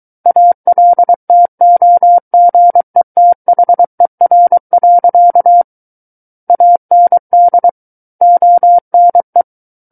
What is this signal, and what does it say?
Morse code, heard as a keyed tone, about 23 wpm, 700 Hz. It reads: ALTOGETHER. AND ONE
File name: Book One / 14-Chapter 14 - 23 WPM